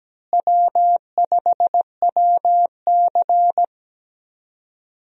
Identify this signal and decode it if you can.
Morse code, heard as a keyed tone, about 17 wpm, 700 Hz. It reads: W5WC
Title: USA Call Signs - 4 Characters - Repeated 3x - 17wpm